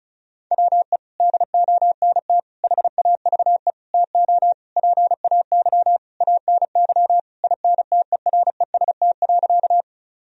Morse code, heard as a keyed tone, about 34 wpm, 700 Hz. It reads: WE DONT HAVE TO PAY ANY INTEREST.
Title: Sentences from Top 1000 Words - Review Part 2 - 35wpm